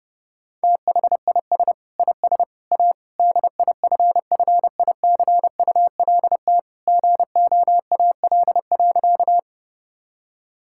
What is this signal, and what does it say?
Morse code, heard as a keyed tone, about 32 wpm, 700 Hz. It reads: THIS IS A DIFFICULT GOAL.
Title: Sentences from Top 1000 Words - Review Part 5 - 30wpm